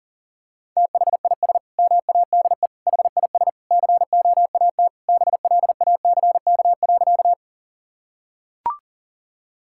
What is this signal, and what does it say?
Morse code, heard as a keyed tone, about 40 wpm, 700 Hz. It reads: THIS MADE HIS COAT BLACK.
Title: Sentences from Top 1600 Words 40wpm